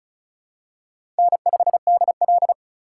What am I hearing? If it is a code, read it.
Morse code, heard as a keyed tone, about 36 wpm, 700 Hz. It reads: N5DL
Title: USA Call Signs - 4 Characters - 4x Character Spacing - 35wpm